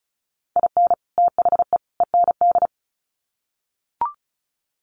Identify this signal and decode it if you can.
Morse code, heard as a keyed tone, about 34 wpm, 700 Hz. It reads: IN THE END
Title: Sets of 3 Words 35wpm